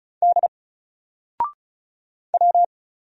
Morse code, heard as a keyed tone, about 35 wpm, 700 Hz. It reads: D W
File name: Single-Letter-Number-35wpm